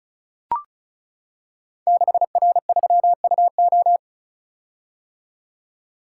Morse code, heard as a keyed tone, about 36 wpm, 700 Hz. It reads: 6R3UO
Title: Code Groups - 5 Letter:Numbers - 35wpm